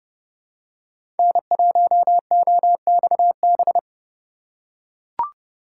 Morse code, heard as a keyed tone, about 30 wpm, 700 Hz. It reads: N1OXB
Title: USA Call Signs - 5 Characters - 3x Character Spacing - 30wpm